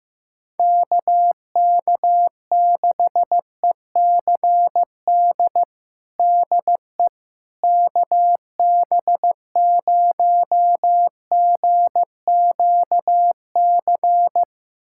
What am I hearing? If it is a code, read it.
Morse code, heard as a keyed tone, about 16 wpm, 700 Hz. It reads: KK6ECD DE KB0GQC
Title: USA Callsign DE Callsign - 6 Characters - 15wpm